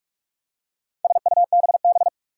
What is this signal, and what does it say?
Morse code, heard as a keyed tone, about 46 wpm, 700 Hz. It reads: SUBB
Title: Four Challenging Letter-Number Combinations 45wpm